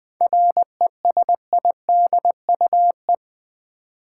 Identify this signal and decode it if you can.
Morse code, heard as a keyed tone, about 20 wpm, 700 Hz. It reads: RESIDUE
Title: 7 Letter Words - First 3 Letters Given - 20wpm